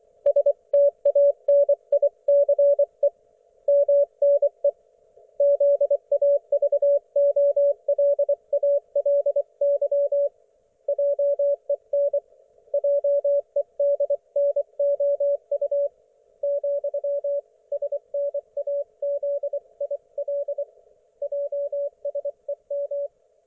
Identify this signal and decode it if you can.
Morse code, heard as a keyed tone, about 24 wpm, 570 Hz. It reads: STANICE MNE ZAVOLALY JEN JEDNOU , SNAZIL JSEM
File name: Archiv / CUC224 / ok1fhi cul224